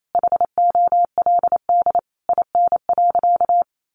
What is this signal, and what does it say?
Morse code, heard as a keyed tone, about 28 wpm, 700 Hz. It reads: HOLD IN.
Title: Sentences from Top 400 Words 28wpm